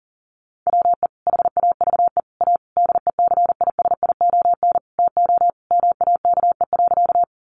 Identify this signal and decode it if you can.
Morse code, heard as a keyed tone, about 37 wpm, 700 Hz. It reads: WE HAVE A DECISION TO MAKE.
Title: Sentences from Top 1000 Words 40wpm